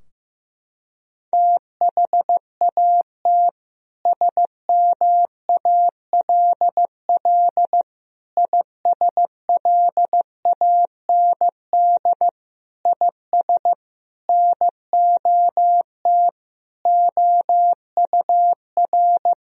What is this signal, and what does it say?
Morse code, heard as a keyed tone, about 15 wpm, 700 Hz. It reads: THAT SMALL ISLAND IS NOT OUR
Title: Sentences from Top 1000 Words - Review Part 4 - 15wpm